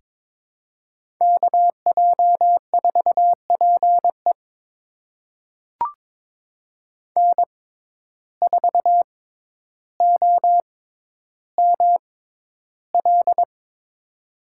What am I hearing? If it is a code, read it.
Morse code, heard as a keyed tone, about 22 wpm, spaced as for 11 wpm, 700 Hz. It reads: KJ4PE N4OML
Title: USA Call Signs - 5 Characters - 6x Character Spacing - 22wpm